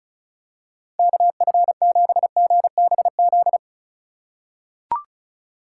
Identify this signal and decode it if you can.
Morse code, heard as a keyed tone, about 36 wpm, 700 Hz. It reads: KF7GBZ
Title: USA Call Signs - 6 Characters - 2x Character Spacing - 35wpm